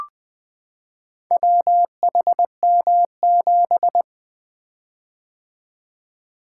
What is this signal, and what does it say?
Morse code, heard as a keyed tone, about 21 wpm, 700 Hz. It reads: WHM7
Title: Four Letter-Number Combinations - 20wpm